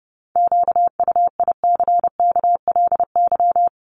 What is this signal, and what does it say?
Morse code, heard as a keyed tone, about 30 wpm, 700 Hz. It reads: QUICKLY
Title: LY Suffix - Top 100 Words with Base Words 30wpm